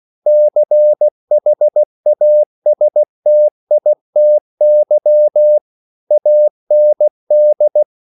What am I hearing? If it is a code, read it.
Morse code, heard as a keyed tone, about 16 wpm, 600 Hz. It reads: CHASTITY AND
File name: CW / Practice / EnglishQuotations0116